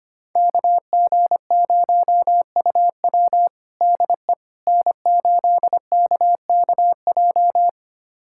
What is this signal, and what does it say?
Morse code, heard as a keyed tone, about 25 wpm, 700 Hz. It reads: KG0UW DE N8KKJ
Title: USA Callsign DE Callsign - 5 Characters - 25wpm